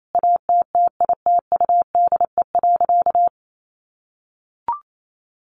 Morse code, heard as a keyed tone, about 28 wpm, 700 Hz. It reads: ATTITUDE.
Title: Sentences from Top 1300 Words 28wpm